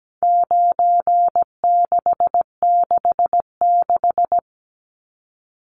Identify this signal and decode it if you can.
Morse code, heard as a keyed tone, about 17 wpm, 700 Hz. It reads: 9666
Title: Numbers - 4-Digit 17wpm